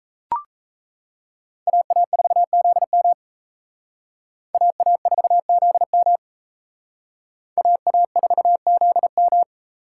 Morse code, heard as a keyed tone, about 34 wpm, 700 Hz. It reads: AA4ZM AA4ZM AA4ZM
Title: Speed Racing - USA Call Signs 28wpm